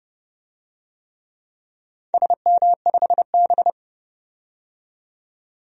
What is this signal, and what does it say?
Morse code, heard as a keyed tone, about 30 wpm, 700 Hz. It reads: SM5B
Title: DXCC-100 Call Signs - 4 Characters - 30wpm